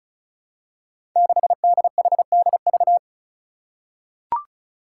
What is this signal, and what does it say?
Morse code, heard as a keyed tone, about 36 wpm, 700 Hz. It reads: 6DHDV
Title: Five Challenging Letter-Number Combinations 35wpm